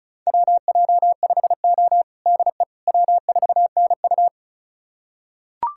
Morse code, heard as a keyed tone, about 36 wpm, 700 Hz. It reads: WJ5O DE W4NU
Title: USA Callsign DE Callsign - 4 Characters - 35wpm